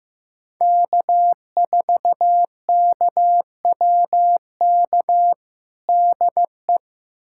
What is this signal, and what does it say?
Morse code, heard as a keyed tone, about 15 wpm, 700 Hz. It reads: K4KWK DE
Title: USA Callsign DE Callsign - 5 Characters - 15wpm